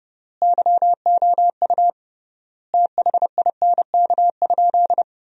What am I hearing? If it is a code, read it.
Morse code, heard as a keyed tone, about 32 wpm, 700 Hz. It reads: YOU THINK?
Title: 3x Word-Spacing - Sentences from Top 200 Words - 30wpm